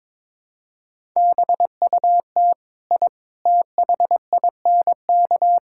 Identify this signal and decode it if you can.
Morse code, heard as a keyed tone, about 22 wpm, 700 Hz. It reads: BUT I THINK
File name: practice / 22wpm / Sentences with 3-Word Phrases - 22wpm